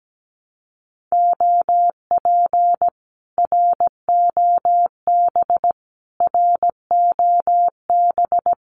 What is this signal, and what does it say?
Morse code, heard as a keyed tone, about 17 wpm, 700 Hz. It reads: OP ROB ROB
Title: Common QSO Phrases Encore 17wpm